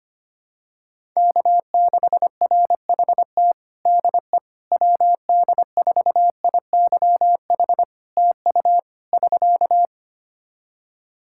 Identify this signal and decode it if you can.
Morse code, heard as a keyed tone, about 25 wpm, 700 Hz. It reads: K6RHT DE WD4IYH TU <SK>
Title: Practice QSO - Exchange 4 - Ending - 25wpm